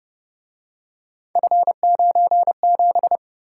Morse code, heard as a keyed tone, about 29 wpm, 700 Hz. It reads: F97